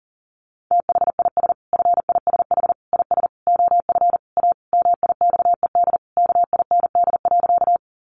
Morse code, heard as a keyed tone, about 40 wpm, 700 Hz. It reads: THIS FISH IS OF A MIXED KIND.
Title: Sentences from Top 1500 Words 40wpm